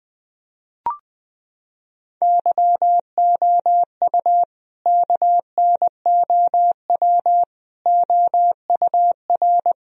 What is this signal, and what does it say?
Morse code, heard as a keyed tone, about 20 wpm, 700 Hz. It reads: YOU KNOW OUR
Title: Sentences from Top 1000 Words - Review Part 4 - 20wpm